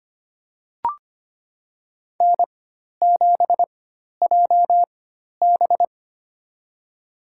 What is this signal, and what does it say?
Morse code, heard as a keyed tone, about 25 wpm, spaced as for 12 wpm, 700 Hz. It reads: N7JB
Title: USA Call Signs - 4x Character Spacing - 25wpm